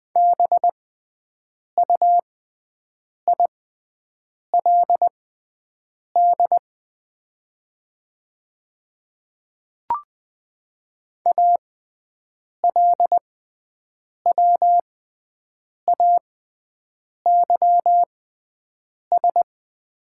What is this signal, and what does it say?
Morse code, heard as a keyed tone, about 20 wpm, spaced as for 7 wpm, 690 Hz. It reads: BUILD ALWAYS
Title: 6x Character Spacing - Top 500 Words - 20wpm